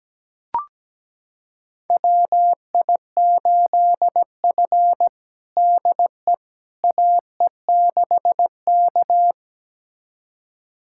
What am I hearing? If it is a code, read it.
Morse code, heard as a keyed tone, about 17 wpm, 700 Hz. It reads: WI8F DE AE6K
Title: USA Callsign DE Callsign - 4 Characters - 17wpm